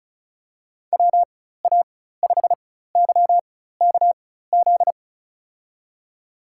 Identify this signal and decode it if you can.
Morse code, heard as a keyed tone, about 34 wpm, spaced as for 16 wpm, 690 Hz. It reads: WA5YKZ
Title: USA Call Signs - 4x Character Spacing - 35wpm